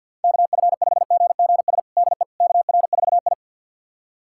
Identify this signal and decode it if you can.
Morse code, heard as a keyed tone, about 51 wpm, 700 Hz. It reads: KF5GGS DE KR4I
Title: USA Callsign DE Callsign 50wpm